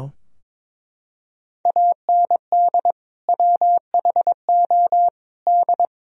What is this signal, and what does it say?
Morse code, heard as a keyed tone, about 22 wpm, 700 Hz. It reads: AND WHO D
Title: Sentences from Top 1000 Words - Review Part 3 - 22wpm